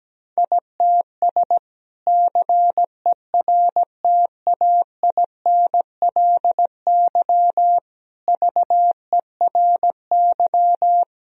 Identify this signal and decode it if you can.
Morse code, heard as a keyed tone, about 17 wpm, 700 Hz. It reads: ITS CERTAINLY VERY
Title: Sentences from Top 1000 Words - Review Part 2 - 17wpm